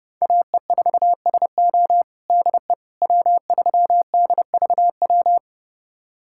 Morse code, heard as a keyed tone, about 31 wpm, 700 Hz. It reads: AE4SO DE W3DVW
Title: USA Callsign DE Callsign - 5 Characters - 30wpm